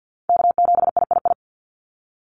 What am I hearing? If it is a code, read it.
Morse code, heard as a keyed tone, about 50 wpm, 700 Hz. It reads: K7III
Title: NA SOTA Chaser Callsigns 50wpm